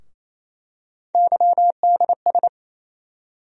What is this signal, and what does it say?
Morse code, heard as a keyed tone, about 28 wpm, 700 Hz. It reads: YDS
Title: USA Call Sign Postfixes - 3 Letters 28wpm